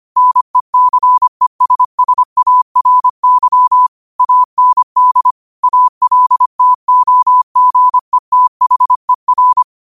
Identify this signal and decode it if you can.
Morse code, heard as a keyed tone, about 25 wpm, 1000 Hz. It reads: NECESSARY AND ALTOGETHER
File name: qotdfiles / 25wpm